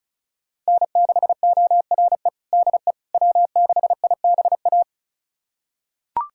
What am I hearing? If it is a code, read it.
Morse code, heard as a keyed tone, about 36 wpm, 700 Hz. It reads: N6ORE DE W6IBA
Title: USA Callsign DE Callsign - 5 Characters - 35wpm